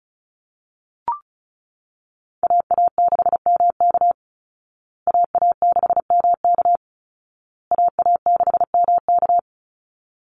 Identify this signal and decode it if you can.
Morse code, heard as a keyed tone, about 34 wpm, 700 Hz. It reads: AA6MK AA6MK AA6MK
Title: Call Signs Repeated 2x - 35wpm